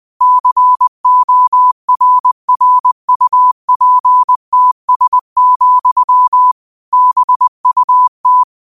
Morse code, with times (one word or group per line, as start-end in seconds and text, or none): 0.2-6.5 CORRUPTS,
6.9-8.4 BUT